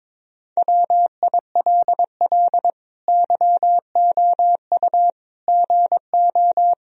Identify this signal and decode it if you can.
Morse code, heard as a keyed tone, about 21 wpm, 700 Hz. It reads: WILL YOU GO
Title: Sentences from Top 1000 Words - Review Part 3 - 22wpm